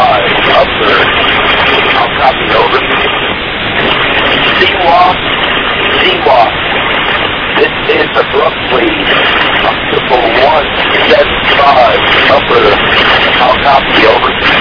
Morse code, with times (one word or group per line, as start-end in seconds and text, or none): none